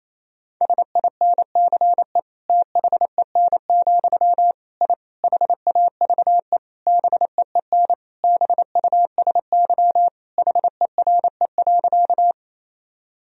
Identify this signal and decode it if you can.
Morse code, heard as a keyed tone, about 28 wpm, 700 Hz. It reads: SINCE THEN, I HAVE BEEN BUSY HERE.
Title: Sentences from Top 1000 Words - Review Part 3 - 28wpm